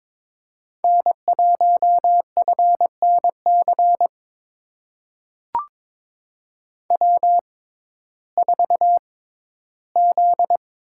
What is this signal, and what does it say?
Morse code, heard as a keyed tone, about 22 wpm, spaced as for 14 wpm, 700 Hz. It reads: N1FNC W4Z
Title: USA Call Signs - 5 Characters - 6x Character Spacing - 22wpm